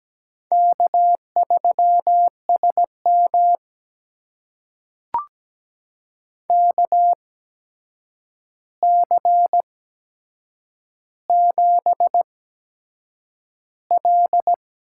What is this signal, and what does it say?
Morse code, heard as a keyed tone, about 17 wpm, spaced as for 7 wpm, 700 Hz. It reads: K3SM KC7L
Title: USA Call Signs - 8x Character Spacing - 17wpm